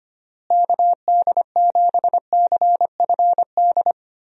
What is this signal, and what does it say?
Morse code, heard as a keyed tone, about 25 wpm, 700 Hz. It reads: KD7CFD
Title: USA Call Signs - 6 Characters - Repeated 3x - 25wpm